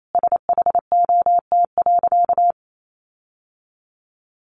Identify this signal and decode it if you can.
Morse code, heard as a keyed tone, about 28 wpm, 700 Hz. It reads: SHOT.
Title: Sentences from Top 1100 Words 28wpm